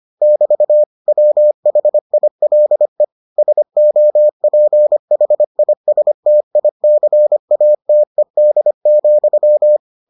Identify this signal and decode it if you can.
Morse code, heard as a keyed tone, about 25 wpm, 600 Hz. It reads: <BT> WHILE SOPHISTICATED,